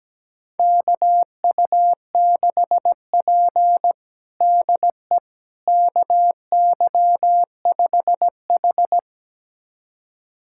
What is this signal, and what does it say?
Morse code, heard as a keyed tone, about 17 wpm, 700 Hz. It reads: KU6P DE KY5H
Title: USA Callsign DE Callsign - 4 Characters - 17wpm